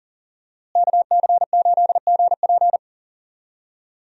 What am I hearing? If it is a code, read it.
Morse code, heard as a keyed tone, about 40 wpm, 700 Hz. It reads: KC8GP
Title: USA Call Signs - 5 Characters - Repeated 3x - 40wpm